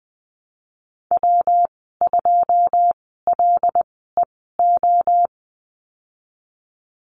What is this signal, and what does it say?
Morse code, heard as a keyed tone, about 20 wpm, spaced as for 15 wpm, 700 Hz. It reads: W2LEO
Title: 140 Lesson 23g - Letter f - Call Signs 20wpm